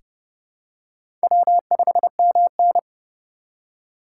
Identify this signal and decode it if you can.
Morse code, heard as a keyed tone, about 28 wpm, 700 Hz. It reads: W5MN
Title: USA Call Signs - 4 Characters - 8x Character Spacing - 30wpm